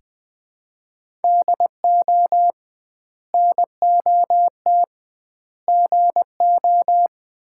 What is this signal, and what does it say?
Morse code, heard as a keyed tone, about 20 wpm, 700 Hz. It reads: DO NOT GO
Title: 2x Word-Spacing - Sets of 4 Words - 20wpm